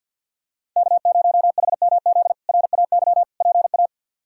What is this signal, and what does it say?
Morse code, heard as a keyed tone, about 49 wpm, 690 Hz. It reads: K0HMZ RAY PA
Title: K1USN SST QSO Practice - 50wpm